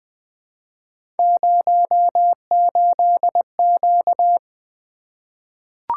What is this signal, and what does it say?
Morse code, heard as a keyed tone, about 20 wpm, 700 Hz. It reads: 08Q
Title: Three Letter-Number Combinations - 20wpm